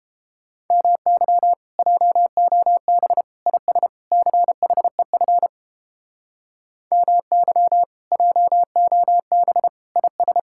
Morse code, heard as a keyed tone, about 32 wpm, 700 Hz. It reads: MY JOB IS CHEF MY JOB IS